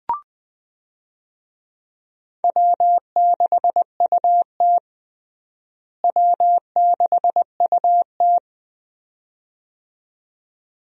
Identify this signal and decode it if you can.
Morse code, heard as a keyed tone, about 20 wpm, 700 Hz. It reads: W6UT W6UT
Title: USA Call Signs - 4 Characters - Repeated 2x - 20wpm